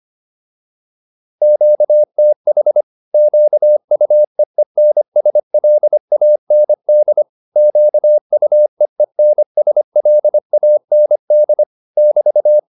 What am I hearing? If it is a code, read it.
Morse code, heard as a keyed tone, about 25 wpm, 600 Hz. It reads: QTH QUEENSLAND QUEENSLAND <BT>
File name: QSO407 25